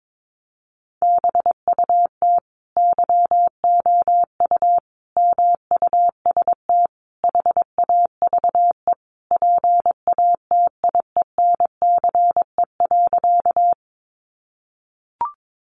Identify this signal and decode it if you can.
Morse code, heard as a keyed tone, about 22 wpm, 700 Hz. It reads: BUT YOU MUST HAVE PATIENCE.
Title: Sentences from Top 1300 Words 22wpm